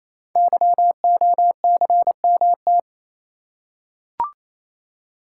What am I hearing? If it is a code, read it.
Morse code, heard as a keyed tone, about 29 wpm, 700 Hz. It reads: YOCMT
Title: Code Groups - 5 Letters - 28wpm